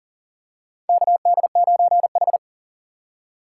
Morse code, heard as a keyed tone, about 37 wpm, 700 Hz. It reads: KD9H
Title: USA Call Signs - 4 Characters - 2x Character Spacing - 40wpm